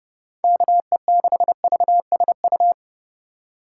Morse code, heard as a keyed tone, about 32 wpm, 700 Hz. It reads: KE6VSU
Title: USA Call Signs - 6 Characters - 4x Character Spacing - 30wpm